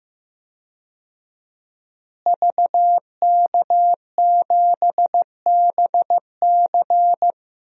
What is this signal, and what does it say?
Morse code, heard as a keyed tone, about 14 wpm, 700 Hz. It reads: VK7BC
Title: DXCC-100 Call Signs - 5 Characters - Repeated 3x - 15wpm